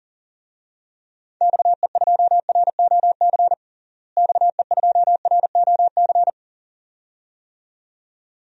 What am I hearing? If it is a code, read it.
Morse code, heard as a keyed tone, about 40 wpm, 700 Hz. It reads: XE2ROC XE2ROC
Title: DXCC-100 Call Signs - 6 Characters - Repeated 2x - 40wpm